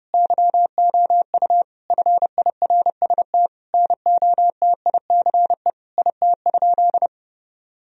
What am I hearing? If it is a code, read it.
Morse code, heard as a keyed tone, about 32 wpm, 700 Hz. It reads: YOU FIRST NOTICE IT?
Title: Sentences from Top 1000 Words - Review Part 2 - 30wpm